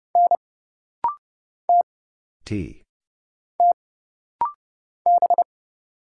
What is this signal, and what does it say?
Morse code, heard as a keyed tone, about 31 wpm, 700 Hz. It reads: N T T B
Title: Single Letter-Number - Rapid-Fire 30wpm